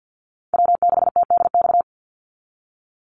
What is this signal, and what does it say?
Morse code, heard as a keyed tone, about 49 wpm, 700 Hz. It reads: W6TDX
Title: NA SOTA Chaser Callsigns 50wpm